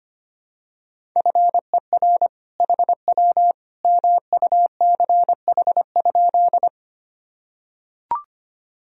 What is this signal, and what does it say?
Morse code, heard as a keyed tone, about 25 wpm, 700 Hz. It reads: FER HW MUCH?
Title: Sentences from Top 200 Words and Abbreviations 25wpm